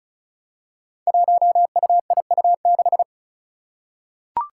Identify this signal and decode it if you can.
Morse code, heard as a keyed tone, about 36 wpm, 700 Hz. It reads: 1UIU6
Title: Five Challenging Letter-Number Combinations 35wpm